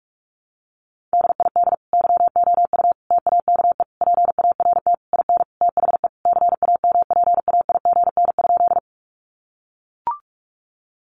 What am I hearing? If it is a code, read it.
Morse code, heard as a keyed tone, about 44 wpm, 700 Hz. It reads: DID YOU TAKE PART IN THE CAMPAIGN?
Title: Sentences from Top 1200 Words 45wpm